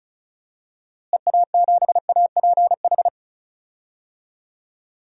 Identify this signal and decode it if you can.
Morse code, heard as a keyed tone, about 34 wpm, 700 Hz. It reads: EA7APH
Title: DXCC-100 Call Signs - 6 Characters - Repeated 3x - 35wpm